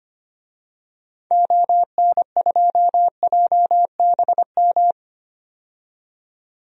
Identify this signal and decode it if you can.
Morse code, heard as a keyed tone, about 25 wpm, 700 Hz. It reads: ON2JBM
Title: DXCC-100 Call Signs - 6 Characters - 25wpm